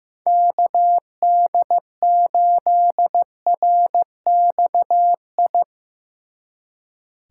Morse code, heard as a keyed tone, about 17 wpm, 700 Hz. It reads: KD8RXI
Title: Speed Racing - USA 2x3 Call Signs 15wpm